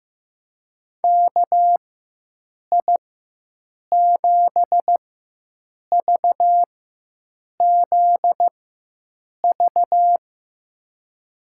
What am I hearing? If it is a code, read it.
Morse code, heard as a keyed tone, about 15 wpm, spaced as for 7 wpm, 700 Hz. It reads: KI7VZV
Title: USA Call Signs - 6 Characters - 4x Character Spacing - 15wpm